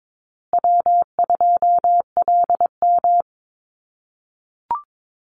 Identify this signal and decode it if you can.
Morse code, heard as a keyed tone, about 22 wpm, 700 Hz. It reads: W2LM
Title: USA Call Signs Repeated 3x 22wpm